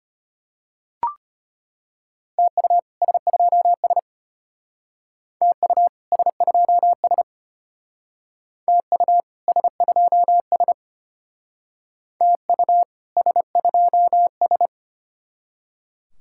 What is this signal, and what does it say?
Morse code, heard as a keyed tone, about 32 wpm, 700 Hz. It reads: TU S2S TU S2S TU S2S TU S2S